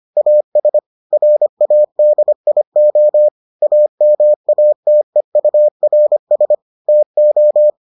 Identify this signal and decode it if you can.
Morse code, heard as a keyed tone, about 25 wpm, 600 Hz. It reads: AS RADIO AMATEURS TO